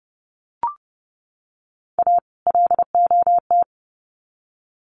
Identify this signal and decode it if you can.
Morse code, heard as a keyed tone, about 30 wpm, 700 Hz. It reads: A LOT
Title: Sets of 2 Words 30wpm